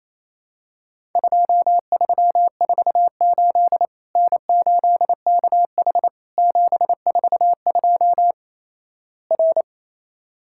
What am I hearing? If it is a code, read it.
Morse code, heard as a keyed tone, about 28 wpm, 700 Hz. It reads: 2348 N8KH 742 R
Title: Worked All Europe DX Contest - QTC Exchange - 28wpm